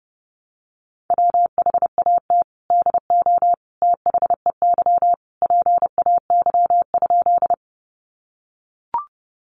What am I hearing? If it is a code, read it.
Morse code, heard as a keyed tone, about 30 wpm, 700 Hz. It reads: WHAT DO THEY PAY?
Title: Sentences from Top 700 Words 30wpm